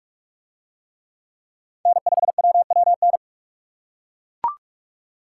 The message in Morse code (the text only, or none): N5WWN